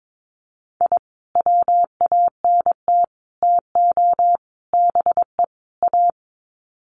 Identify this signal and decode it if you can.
Morse code, heard as a keyed tone, about 22 wpm, 700 Hz. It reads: I WANT TO BE A
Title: Sets of 5 Words Encore 22wpm